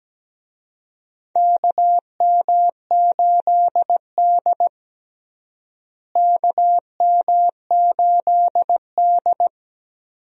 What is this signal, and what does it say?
Morse code, heard as a keyed tone, about 17 wpm, 700 Hz. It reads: KM8D KM8D
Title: USA Call Signs - 4 Characters - Repeated 2x - 17wpm